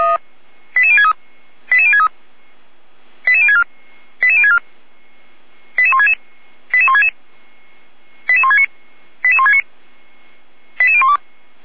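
Five-tone selective call (ZVEI), heard as quick runs of five tones, tones of about 70 ms. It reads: 88063 88063 88064 88064 88166 88166 88169 88169 88012
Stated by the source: ZVEI five-tone selective call 88365 received via SDR.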